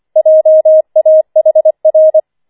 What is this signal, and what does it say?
Morse code, heard as a keyed tone, about 25 wpm, 630 Hz. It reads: JAHR
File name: 25wpm